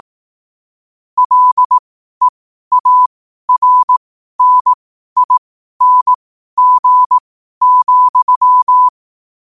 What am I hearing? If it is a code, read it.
Morse code, heard as a keyed tone, about 18 wpm, spaced as for 12 wpm, 1000 Hz. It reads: LEARNING,
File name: 12wpm